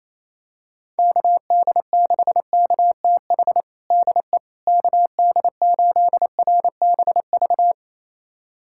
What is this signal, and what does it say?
Morse code, heard as a keyed tone, about 28 wpm, 700 Hz. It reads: KD6KTH DE KD8RBV
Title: USA Callsign DE Callsign - 6 Characters - 28wpm